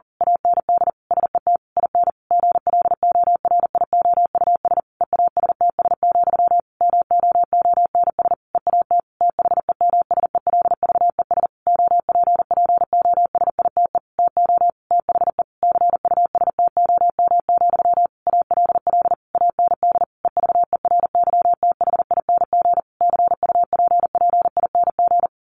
Morse code, heard as a keyed tone, about 40 wpm, 700 Hz. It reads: AND SET IN GLORIOUS EASTS, MOONS EAT THEMSELVES OPPOSITE TO THE CUSTOM, ALL AND EVERYTHING CUPPING
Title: A Sound Of Thunder 40wpm